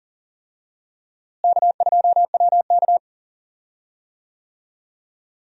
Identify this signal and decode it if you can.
Morse code, heard as a keyed tone, about 37 wpm, 700 Hz. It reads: K2WK
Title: USA Call Signs - 4 Characters - Repeated 3x - 40wpm